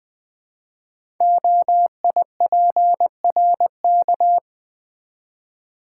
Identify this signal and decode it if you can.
Morse code, heard as a keyed tone, about 20 wpm, 700 Hz. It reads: OIPRK
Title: Code Groups - 5 Letters - 20wpm